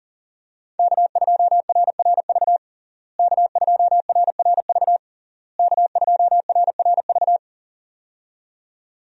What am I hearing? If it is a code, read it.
Morse code, heard as a keyed tone, about 40 wpm, 700 Hz. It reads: K2RRV K2RRV K2RRV
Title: USA Call Signs - 5 Characters - Repeated 3x - 40wpm